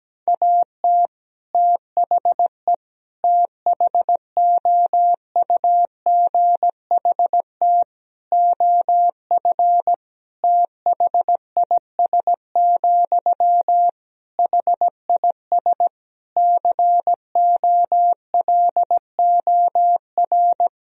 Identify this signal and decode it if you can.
Morse code, heard as a keyed tone, about 17 wpm, 700 Hz. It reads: AT THE THOUGHT OF THIS, HIS COLOR
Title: Sentences from Top 1000 Words - Review Part 2 - 17wpm